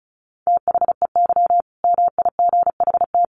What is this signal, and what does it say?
Morse code, heard as a keyed tone, about 34 wpm, 700 Hz. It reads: THEY MIGHT
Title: Sentences from Top 1000 Words 35wpm